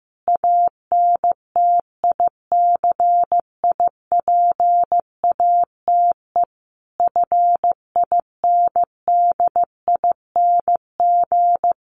Morse code, heard as a keyed tone, about 15 wpm, 700 Hz. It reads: ANTICIPATE FINDING
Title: Sentences from Top 1500 Words 15wpm